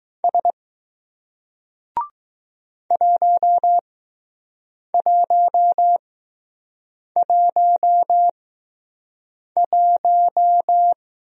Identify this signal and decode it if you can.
Morse code, heard as a keyed tone, about 20 wpm, spaced as for 6 wpm, 700 Hz. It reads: S 1111